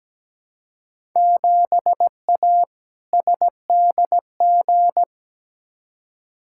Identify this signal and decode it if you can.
Morse code, heard as a keyed tone, about 17 wpm, 700 Hz. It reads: 7A SDG
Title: Field Day Exchanges Encore - 17wpm